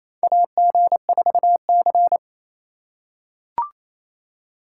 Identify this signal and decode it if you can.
Morse code, heard as a keyed tone, about 28 wpm, 700 Hz. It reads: AG4C
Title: USA Call Signs - 2x Character Spacing - 28wpm